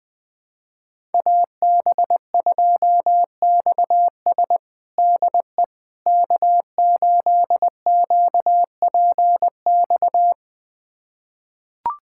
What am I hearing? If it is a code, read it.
Morse code, heard as a keyed tone, about 20 wpm, 700 Hz. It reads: AB2XS DE K8QPX
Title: USA Callsign DE Callsign - 5 Characters - 20wpm